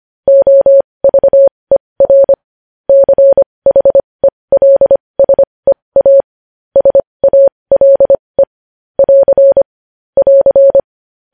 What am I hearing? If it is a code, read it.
Morse code, heard as a keyed tone, about 25 wpm, spaced as for 20 wpm, 550 Hz. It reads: OVER CHELSEA SALE <AR> <AR>
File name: podcast / News headlines-1x-25x20